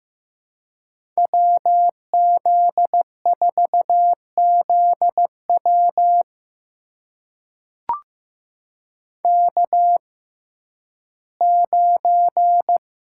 Morse code, heard as a keyed tone, about 15 wpm, spaced as for 11 wpm, 700 Hz. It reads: WZ4ZW K9
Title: USA Call Signs - 5 Characters - 6x Character Spacing - 15wpm